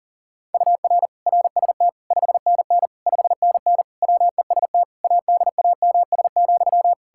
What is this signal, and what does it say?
Morse code, heard as a keyed tone, about 40 wpm, 700 Hz. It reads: UR RST 5NN 5NN WEST ADAMS,
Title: K1USN SST QSO Practice 40wpm